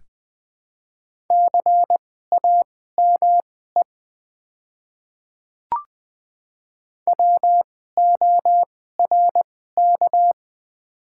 Morse code, heard as a keyed tone, about 20 wpm, spaced as for 14 wpm, 700 Hz. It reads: CAME WORK
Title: 2x Character Spacing - Top 300 Words - 20wpm